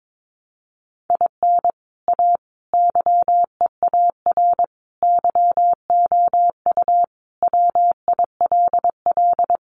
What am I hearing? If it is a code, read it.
Morse code, heard as a keyed tone, about 22 wpm, 700 Hz. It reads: IN A YEAR YOU WILL
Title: Sentences from Top 600 Words 22wpm